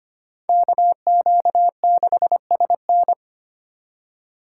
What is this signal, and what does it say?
Morse code, heard as a keyed tone, about 30 wpm, 700 Hz. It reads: KQ6SN
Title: Speed Racing - USA 2x2 Call Signs 25wpm